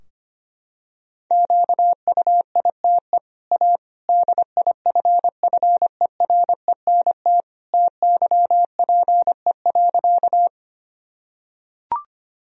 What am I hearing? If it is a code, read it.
Morse code, heard as a keyed tone, about 25 wpm, 700 Hz. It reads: QUITE A DIFFERENT TYPE.
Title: Sentences from Top 1000 Words - Review Part 3 - 25wpm